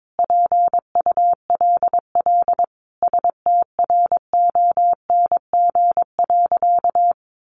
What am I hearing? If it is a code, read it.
Morse code, heard as a keyed tone, about 22 wpm, 700 Hz. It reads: PULL STRONG.
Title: Sentences from Top 1500 Words 22wpm